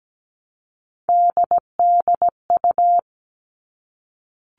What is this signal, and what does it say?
Morse code, heard as a keyed tone, about 17 wpm, 700 Hz. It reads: DDU
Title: All Cut Numbers - 3-Digit 17wpm